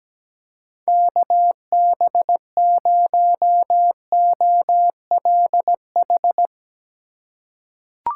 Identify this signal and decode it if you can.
Morse code, heard as a keyed tone, about 17 wpm, 700 Hz. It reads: KB0OLH
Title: USA Call Signs - 6 Characters - 3x Character Spacing - 17wpm